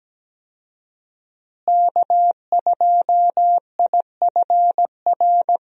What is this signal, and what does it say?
Morse code, heard as a keyed tone, about 17 wpm, 700 Hz. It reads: K2IFR
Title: USA Call Signs - 5 Characters - Repeated 3x - 17wpm